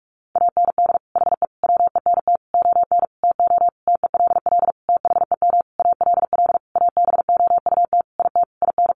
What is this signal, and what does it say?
Morse code, heard as a keyed tone, about 44 wpm, 700 Hz. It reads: AND HE WENT ON TO TELL THEM ALL ABOUT IT IN
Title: Sentences from Top 900 Words 45wpm